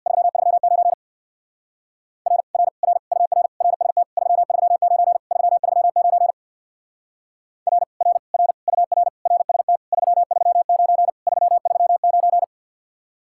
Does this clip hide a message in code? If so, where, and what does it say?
Yes: Morse code from 0.1 to 12.4 s, heard as a keyed tone, about 66 wpm, 700 Hz. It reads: 339 R R R UR RST 339 339 R R R UR RST 339 339